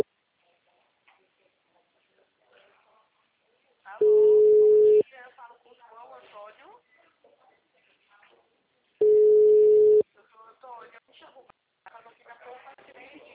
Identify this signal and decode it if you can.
Ringback tone (Europe, 425 Hz), with speech.